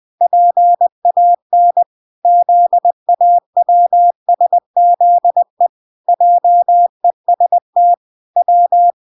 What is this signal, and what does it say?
Morse code, heard as a keyed tone, about 20 wpm, 700 Hz. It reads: PAN ZAWSZE JEST W